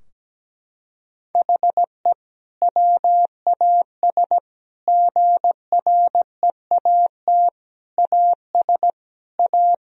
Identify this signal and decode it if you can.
Morse code, heard as a keyed tone, about 17 wpm, 700 Hz. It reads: HE WAS GREAT AS A
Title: Sentences from Top 1000 Words - Review Part 3 - 17wpm